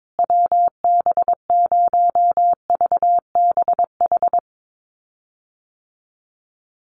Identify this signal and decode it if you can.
Morse code, heard as a keyed tone, about 22 wpm, 700 Hz. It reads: WB0VBH
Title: USA Call Signs Repeated 3x 22wpm